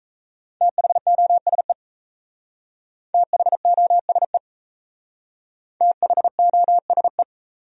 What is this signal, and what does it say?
Morse code, heard as a keyed tone, about 34 wpm, 700 Hz. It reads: THOSE THOSE THOSE
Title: Speed-Racing Top 300 Words 28wpm